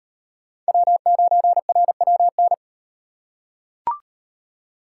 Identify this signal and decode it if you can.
Morse code, heard as a keyed tone, about 35 wpm, 700 Hz. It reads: W9RWN